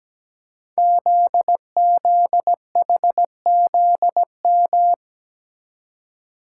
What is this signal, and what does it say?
Morse code, heard as a keyed tone, about 17 wpm, 700 Hz. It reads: ZZHZM
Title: Code Groups - 5 Letters - 17wpm